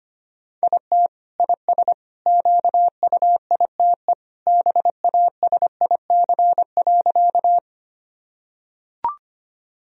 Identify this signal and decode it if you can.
Morse code, heard as a keyed tone, about 25 wpm, 700 Hz. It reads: IT IS QUITE BASIC.
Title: Sentences from Top 1000 Words - Review Part 3 - 25wpm